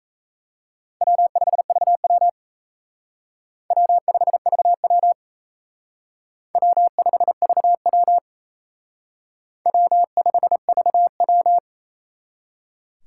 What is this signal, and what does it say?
Morse code, heard as a keyed tone, about 34 wpm, 700 Hz. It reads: W5VW W5VW W5VW W5VW
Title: Speed Racing - USA 1x2 Call Signs 28wpm